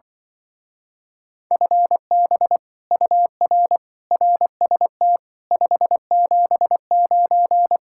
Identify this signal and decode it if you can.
Morse code, heard as a keyed tone, about 25 wpm, 700 Hz. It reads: FB UR RST 579